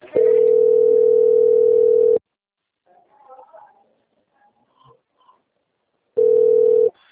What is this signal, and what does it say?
Ringback tone (North America), 440 + 480 Hz.